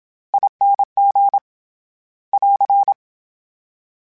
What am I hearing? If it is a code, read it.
Morse code, heard as a keyed tone, about 27 wpm, 800 Hz. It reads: ING <AR>
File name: Special INGs-mp3s / Special INGs-mp3s / SlowToFaster ing @25